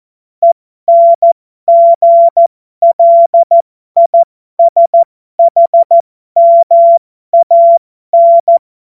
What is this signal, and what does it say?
Morse code, heard as a keyed tone, about 14 wpm, spaced as for 12 wpm, 680 Hz. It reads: ENGLISHMAN
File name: Morsekurs / Witze